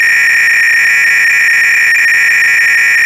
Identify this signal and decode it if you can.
300-baud modem data.
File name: Repository / ACR-Audio-Cassette-Programs / MITS-BASIC-GAME-LUNAR-LANDER-CLOADL